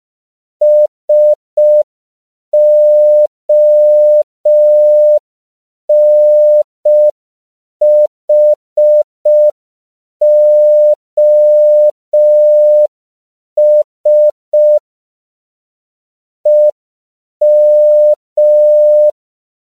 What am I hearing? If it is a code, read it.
Morse code, heard as a keyed tone, about 5 wpm, 600 Hz. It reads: SONHOS EM
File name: 5PPM